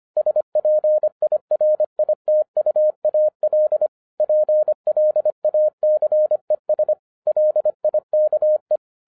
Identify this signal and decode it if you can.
Morse code, heard as a keyed tone, about 26 wpm, 610 Hz. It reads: SPIRITUAL PLACES LIKE
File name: 2011-10-mp3 / n04-JUST MOVED 25wpm